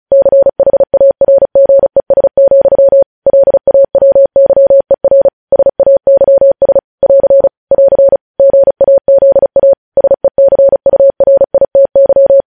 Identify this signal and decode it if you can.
Morse code, heard as a keyed tone, about 36 wpm, 550 Hz. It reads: CHARGES, LAWYER SAYS <AR> <AR> GAZA SECURITY
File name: podcast / News headlines-1x-35x35